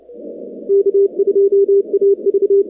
Morse code, heard as a keyed tone, about 30 wpm, 400 Hz. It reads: K2AV
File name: contests / 2019-cw-wpx-kt5j-k5pi- -2330 0-19